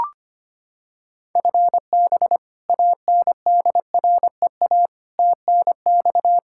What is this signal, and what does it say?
Morse code, heard as a keyed tone, about 25 wpm, 700 Hz. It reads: FB ANDREA TNX
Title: Practice QSO - Exchange 3 - More Info - 25wpm